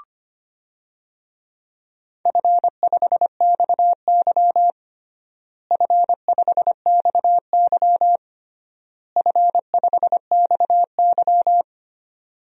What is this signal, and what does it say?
Morse code, heard as a keyed tone, about 25 wpm, 700 Hz. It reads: F5XY F5XY F5XY
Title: DXCC-100 Call Signs - 4 Characters - Repeated 3x - 25wpm